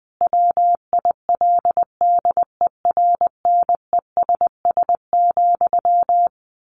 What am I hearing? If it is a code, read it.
Morse code, heard as a keyed tone, about 20 wpm, 700 Hz. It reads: WILDERNESS,
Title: A Sound Of Thunder 20wpm